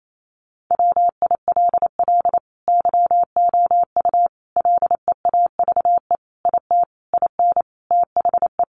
Morse code, heard as a keyed tone, about 28 wpm, 700 Hz. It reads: WILL YOU LEAVE IT IN THE
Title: Sentences from Top 1300 Words 28wpm